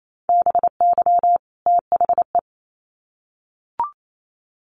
Morse code, heard as a keyed tone, about 28 wpm, 700 Hz. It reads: BY THE
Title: Sets of 2 Words 28wpm